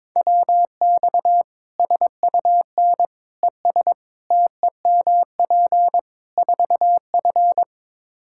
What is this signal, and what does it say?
Morse code, heard as a keyed tone, about 22 wpm, 700 Hz. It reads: WX SUN ES TEMP 4F
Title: Practice QSO - Exchange 2 - Info - 22wpm